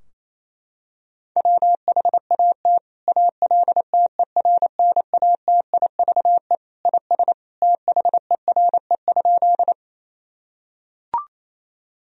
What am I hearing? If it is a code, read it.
Morse code, heard as a keyed tone, about 28 wpm, 700 Hz. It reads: WHAT ALTERNATIVE IS THERE?
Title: Sentences from Top 1000 Words - Review Part 4 - 28wpm